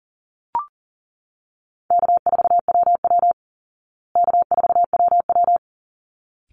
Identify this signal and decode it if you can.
Morse code, heard as a keyed tone, about 40 wpm, 700 Hz. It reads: K4WW K4WW
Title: Easy Call Signs Repeated 2x - 40wpm